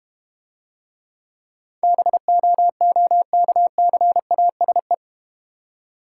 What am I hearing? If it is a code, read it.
Morse code, heard as a keyed tone, about 32 wpm, 700 Hz. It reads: BOOKCASE